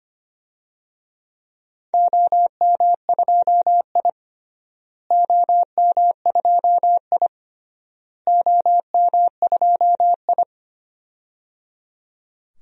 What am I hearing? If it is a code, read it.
Morse code, heard as a keyed tone, about 25 wpm, 700 Hz. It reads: OM2I OM2I OM2I
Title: DXCC-100 Call Signs - 4 Characters - Repeated 3x - 25wpm